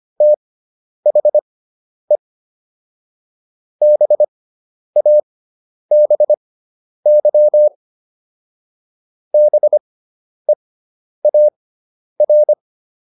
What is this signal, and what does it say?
Morse code, heard as a keyed tone, about 25 wpm, spaced as for 10 wpm, 600 Hz. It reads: THE BABY BEAR